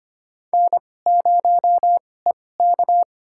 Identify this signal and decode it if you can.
Morse code, heard as a keyed tone, about 25 wpm, spaced as for 18 wpm, 700 Hz. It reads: N0EK
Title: USA Call Signs - 4 Characters - 2x Character Spacing - 25wpm